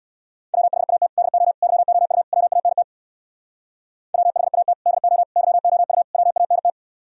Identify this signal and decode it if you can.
Morse code, heard as a keyed tone, about 64 wpm, 700 Hz. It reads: WHAT DO YOU WANT WHAT DO YOU WANT